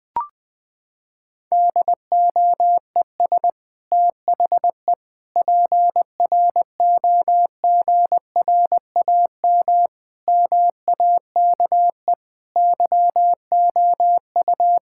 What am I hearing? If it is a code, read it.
Morse code, heard as a keyed tone, about 21 wpm, 700 Hz. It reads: DOES THE PROGRAM MAKE YOU
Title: Sentences from Top 1000 Words - Review Part 5 - 20wpm